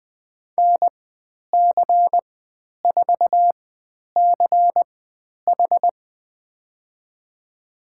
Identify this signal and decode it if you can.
Morse code, heard as a keyed tone, about 20 wpm, spaced as for 10 wpm, 700 Hz. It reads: NC4CH
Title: 102 Lesson 17g - Letter u - Call Signs 20wpm